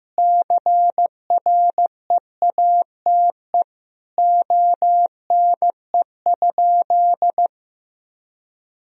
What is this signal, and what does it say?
Morse code, heard as a keyed tone, about 15 wpm, 700 Hz. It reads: CREATE ONE?
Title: Sentences from Top 1000 Words - Review Part 3 - 15wpm